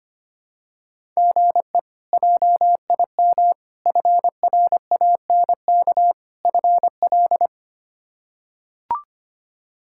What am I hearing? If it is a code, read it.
Morse code, heard as a keyed tone, about 25 wpm, 700 Hz. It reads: GE JIM FRANK FL
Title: K1USN SST QSO Practice - 25wpm